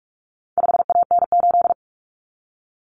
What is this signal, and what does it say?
Morse code, heard as a keyed tone, about 44 wpm, 700 Hz. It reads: 5AN8
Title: Common Cut Numbers - 4-Digit 45wpm